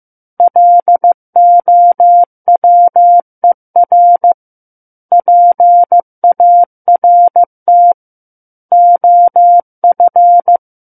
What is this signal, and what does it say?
Morse code, heard as a keyed tone, about 15 wpm, 700 Hz. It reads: LOWER PART OF
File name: Book One / 06-Chapter 6 - 15 WPM